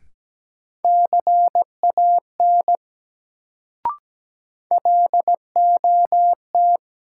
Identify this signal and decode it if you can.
Morse code, heard as a keyed tone, about 17 wpm, 700 Hz. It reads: CAN LOT
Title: Sets of 2 Words - New Word Drill - 17wpm